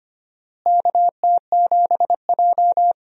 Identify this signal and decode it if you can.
Morse code, heard as a keyed tone, about 25 wpm, 700 Hz. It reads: KT7J
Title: USA Call Signs - 4 Characters - Repeated 2x - 25wpm